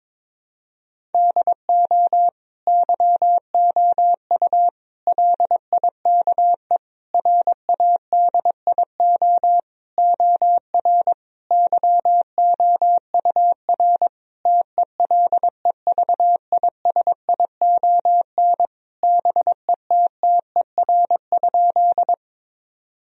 Morse code, heard as a keyed tone, about 22 wpm, 700 Hz. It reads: DO YOU LIKE RADIO OR YOUR TELEVISION BETTER?
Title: Sentences from Top 1000 Words - Review Part 4 - 22wpm